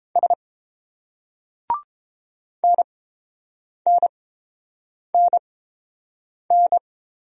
Morse code, heard as a keyed tone, about 30 wpm, spaced as for 7 wpm, 700 Hz. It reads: S NNNN